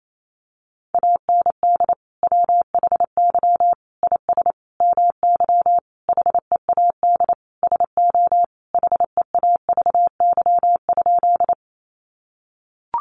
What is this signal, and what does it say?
Morse code, heard as a keyed tone, about 28 wpm, 700 Hz. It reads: AND WHY IS MY HEAD SO HEAVY?
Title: Sentences from Top 1100 Words 28wpm